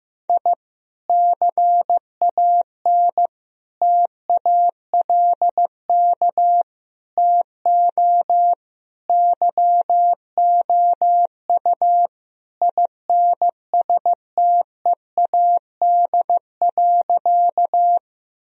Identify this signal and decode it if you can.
Morse code, heard as a keyed tone, about 15 wpm, 700 Hz. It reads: I CAN TALK TO YOU INSTEAD.
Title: Sentences from Top 1000 Words - Review Part 4 - 15wpm